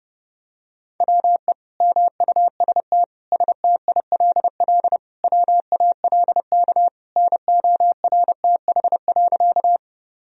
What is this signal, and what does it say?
Morse code, heard as a keyed tone, about 30 wpm, 700 Hz. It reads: WE MUST STILL WALK NORTH.
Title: Sentences from Top 1000 Words - Review Part 1 - 30wpm